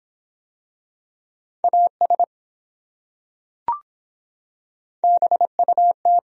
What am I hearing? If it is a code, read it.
Morse code, heard as a keyed tone, about 24 wpm, 700 Hz. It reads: AS BUT